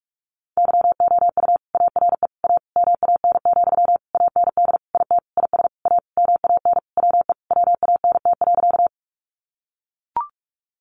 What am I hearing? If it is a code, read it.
Morse code, heard as a keyed tone, about 44 wpm, 700 Hz. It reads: YOU ARE A MAN, AND IT IS A MAN WE WANT.
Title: Sentences from Top 200 Words 45wpm